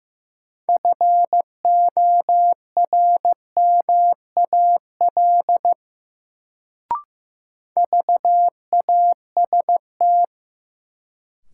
Morse code, heard as a keyed tone, about 15 wpm, 700 Hz. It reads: FORMAL VAST
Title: Sentences from Top 1300 Words - New Word Drill - 15wpm